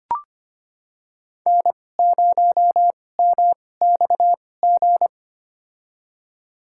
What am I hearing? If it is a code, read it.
Morse code, heard as a keyed tone, about 25 wpm, spaced as for 18 wpm, 700 Hz. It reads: N0MXG
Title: USA Call Signs - 5 Characters - 2x Character Spacing - 25wpm